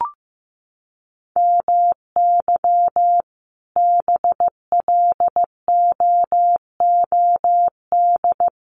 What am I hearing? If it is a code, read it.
Morse code, heard as a keyed tone, about 15 wpm, 700 Hz. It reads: MY BLOOD
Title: Sentences from Top 1100 Words 15wpm